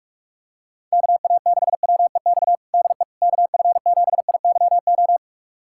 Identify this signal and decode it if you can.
Morse code, heard as a keyed tone, about 47 wpm, 700 Hz. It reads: KA6WEX DE KF7IYO
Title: USA Callsign DE Callsign - 6 Characters - 45wpm